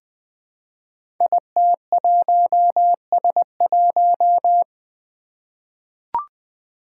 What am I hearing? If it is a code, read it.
Morse code, heard as a keyed tone, about 20 wpm, 700 Hz. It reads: IT1S1
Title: Five Challenging Letter-Number Combinations 20wpm